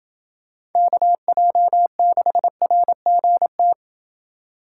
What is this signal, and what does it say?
Morse code, heard as a keyed tone, about 28 wpm, 700 Hz. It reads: KJ6RGT